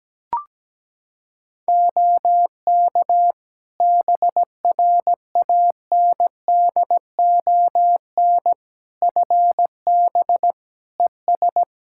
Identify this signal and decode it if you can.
Morse code, heard as a keyed tone, about 17 wpm, 700 Hz. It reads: OK BRANDON FB ES
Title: Practice QSO - Exchange 2 - Info - 17wpm